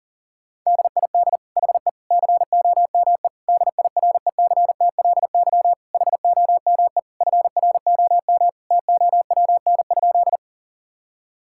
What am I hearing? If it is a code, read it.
Morse code, heard as a keyed tone, about 40 wpm, 700 Hz. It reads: DID HE COME DIRECTLY HOME FROM TOWN?
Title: Sentences from Top 1000 Words - Review Part 5 - 40wpm